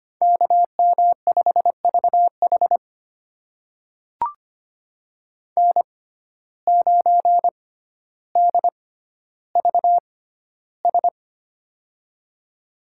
Morse code, heard as a keyed tone, about 25 wpm, spaced as for 13 wpm, 700 Hz. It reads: KM5VH N9DVS
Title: USA Call Signs - 5 Characters - 6x Character Spacing - 25wpm